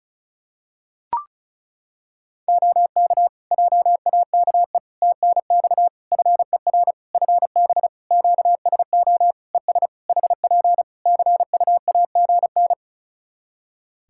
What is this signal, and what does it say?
Morse code, heard as a keyed tone, about 36 wpm, 700 Hz. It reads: OK JAKE TNX FER FB QSO ES HP CUAGN
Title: Practice QSO - Exchange 4 - Ending - 35wpm